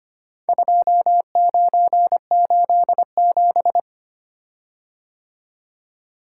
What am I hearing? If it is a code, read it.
Morse code, heard as a keyed tone, about 25 wpm, 700 Hz. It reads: 2987
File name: practice / 25wpm / Four Letter-Number Combinations - 25wpm